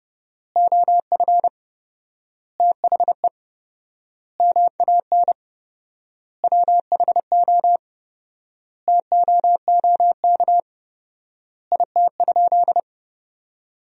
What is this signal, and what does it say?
Morse code, heard as a keyed tone, about 30 wpm, 700 Hz. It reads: OF THE MAN WHO TOOK IT?
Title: 4x Word-Spacing - Sentences from Top 300 Words - 30wpm